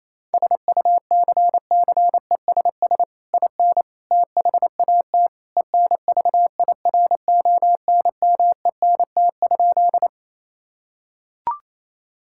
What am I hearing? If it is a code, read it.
Morse code, heard as a keyed tone, about 28 wpm, 700 Hz. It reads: SUCCESS IN THAT ENVIRONMENT?
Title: Sentences from Top 1000 Words - Review Part 5 - 28wpm